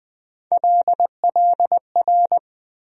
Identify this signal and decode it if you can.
Morse code, heard as a keyed tone, about 20 wpm, 700 Hz. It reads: LLR
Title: USA Call Sign Postfixes - 3 Letters 20wpm